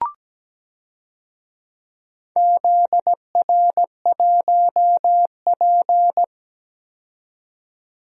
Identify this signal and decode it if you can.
Morse code, heard as a keyed tone, about 17 wpm, 700 Hz. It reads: ZR1P
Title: DXCC-100 Call Signs - 4 Characters - 17wpm